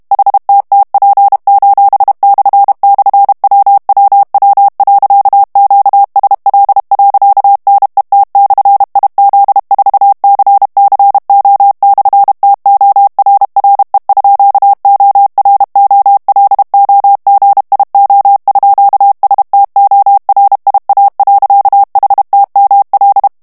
Morse code, heard as a keyed tone, about 32 wpm, 800 Hz. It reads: HTTP://WWW.QSL.NET/IZ4CCO/TORRE_OROLOGIO_STORIA.HTML
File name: morse mp3 / ITALIEN texte clair 1 vitesse 30 mots minute